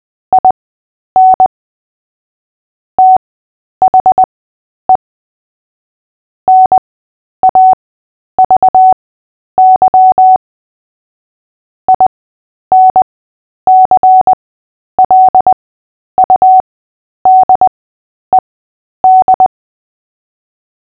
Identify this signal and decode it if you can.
Morse code, heard as a keyed tone, about 20 wpm, spaced as for 10 wpm, 750 Hz. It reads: IN THE NAVY INCLUDED
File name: cw / fists / Short-Story-3-10wpm